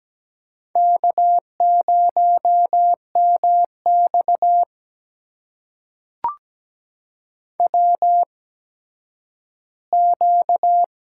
Morse code, heard as a keyed tone, about 17 wpm, spaced as for 10 wpm, 700 Hz. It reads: K0MX WQ
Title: USA Call Signs - 4 Characters - 8x Character Spacing - 17wpm